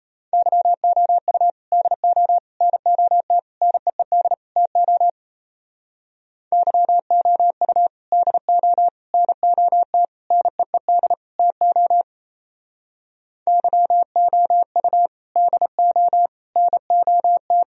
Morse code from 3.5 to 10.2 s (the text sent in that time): NEED TO YOU DO NOT